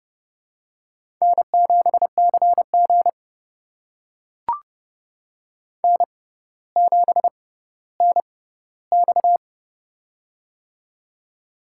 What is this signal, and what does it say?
Morse code, heard as a keyed tone, about 32 wpm, spaced as for 16 wpm, 700 Hz. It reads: N7CG N7NX
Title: USA Call Signs - 6x Character Spacing - 30wpm